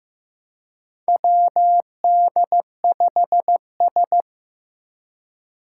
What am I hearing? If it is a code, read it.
Morse code, heard as a keyed tone, about 15 wpm, 700 Hz. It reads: WD5S
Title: Four Challenging Letter-Number Combinations 15wpm